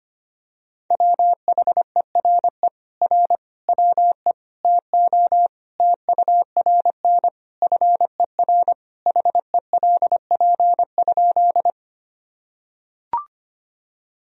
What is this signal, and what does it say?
Morse code, heard as a keyed tone, about 25 wpm, 700 Hz. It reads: WHERE R WE TO TURN FER HELP?
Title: Sentences from Top 200 Words and Abbreviations 25wpm